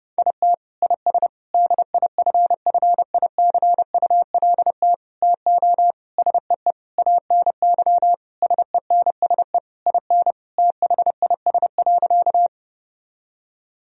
Morse code, heard as a keyed tone, about 30 wpm, 700 Hz. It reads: IT IS DIFFICULT TO SEE ANY SENSE IN THIS.
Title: Sentences from Top 1000 Words - Review Part 4 - 30wpm